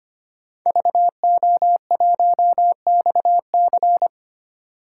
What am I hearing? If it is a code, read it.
Morse code, heard as a keyed tone, about 25 wpm, 700 Hz. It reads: VO1XC
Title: K1USN SST QSO Practice - 25wpm